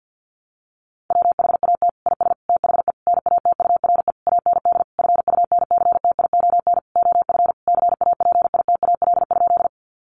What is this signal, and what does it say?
Morse code, heard as a keyed tone, about 48 wpm, 700 Hz. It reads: WHAT IS THE NATURE AND FUNCTION OF CAPITAL?
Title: Sentences from Top 1100 Words 50wpm